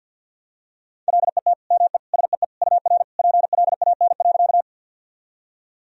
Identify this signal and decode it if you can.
Morse code, heard as a keyed tone, about 47 wpm, 700 Hz. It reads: LET ME SEE UR PLAN.
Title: Sentences from Top 500 Words and Abbreviations 50wpm